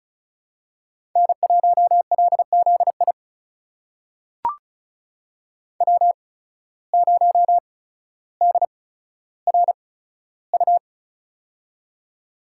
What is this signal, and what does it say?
Morse code, heard as a keyed tone, about 36 wpm, spaced as for 15 wpm, 700 Hz. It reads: N1LZI W0DRU
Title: USA Call Signs - 8x Character Spacing - 35wpm